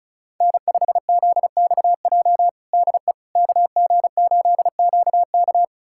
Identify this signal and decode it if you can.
Morse code, heard as a keyed tone, about 36 wpm, 700 Hz. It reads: N5ZXJ DE KG8QK
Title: USA Callsign DE Callsign - 5 Characters - 35wpm